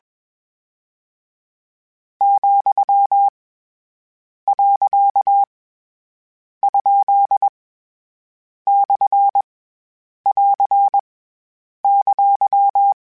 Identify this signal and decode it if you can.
Morse code, heard as a keyed tone, about 21 wpm, 800 Hz. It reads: , . ? / <AR> !
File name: Short Runs / Short Runs-mp3s / Words @20